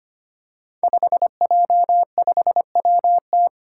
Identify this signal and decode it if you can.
Morse code, heard as a keyed tone, about 25 wpm, 700 Hz. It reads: 5J5WT
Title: Five Challenging Letter-Number Combinations 25wpm